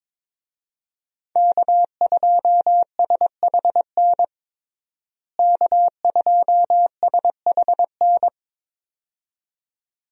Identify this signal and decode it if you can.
Morse code, heard as a keyed tone, about 22 wpm, 700 Hz. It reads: K2SHN K2SHN
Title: USA Call Signs - 5 Characters - Repeated 2x - 22wpm